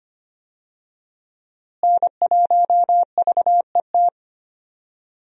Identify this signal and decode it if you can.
Morse code, heard as a keyed tone, about 25 wpm, 700 Hz. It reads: N1VET
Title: USA Call Signs - 5 Characters - Repeated 3x - 25wpm